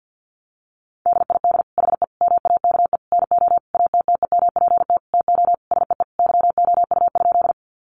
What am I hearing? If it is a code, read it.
Morse code, heard as a keyed tone, about 50 wpm, 700 Hz. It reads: DID HE MAKE NO ATTEMPT TO SEE YOU?
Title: Sentences from Top 1000 Words 50wpm